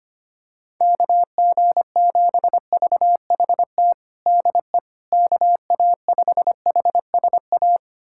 Morse code, heard as a keyed tone, about 25 wpm, 700 Hz. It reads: KG7VHT DE KA5HSA
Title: USA Callsign DE Callsign - 6 Characters - 25wpm